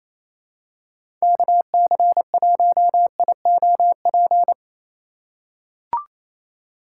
Morse code, heard as a keyed tone, about 28 wpm, 700 Hz. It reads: KC1IOP
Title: USA Call Signs - 6 Characters - 4x Character Spacing - 28wpm